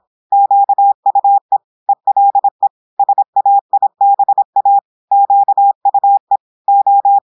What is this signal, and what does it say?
Morse code, heard as a keyed tone, about 26 wpm, 800 Hz. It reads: QUE ELE SAIBA QUE O
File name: portugais texte en clair 2 vitesse 26 mots minutes